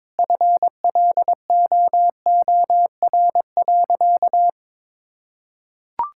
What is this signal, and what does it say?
Morse code, heard as a keyed tone, about 22 wpm, 700 Hz. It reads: FLOOR.
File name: practice / 22wpm / Sentences with 3-Word Phrases - 22wpm